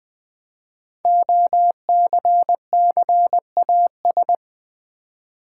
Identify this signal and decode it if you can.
Morse code, heard as a keyed tone, about 20 wpm, 700 Hz. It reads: OCCAS
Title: Build A Word - 5 Letters At A Time - 10-Letter Words - 20wpm